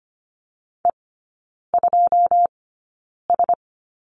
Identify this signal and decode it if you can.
Morse code, heard as a keyed tone, about 25 wpm, spaced as for 9 wpm, 700 Hz. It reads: E2S
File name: Lesson 17e - Letter u - Three Characters 25wpm